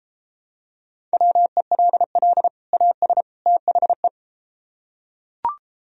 Morse code, heard as a keyed tone, about 33 wpm, 700 Hz. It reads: WELL AS THE